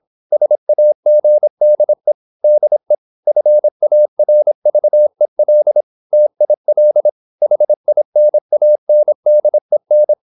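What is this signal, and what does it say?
Morse code, heard as a keyed tone, about 26 wpm, 600 Hz. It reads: SAGDE DE FARVEL TIL HINANDEN
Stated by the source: High-speed (QRQ) Morse at 26 wpm.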